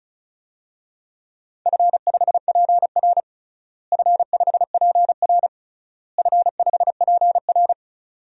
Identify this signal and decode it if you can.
Morse code, heard as a keyed tone, about 36 wpm, 700 Hz. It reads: F5PR F5PR F5PR
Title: DXCC-100 Call Signs - 4 Characters - Repeated 3x - 35wpm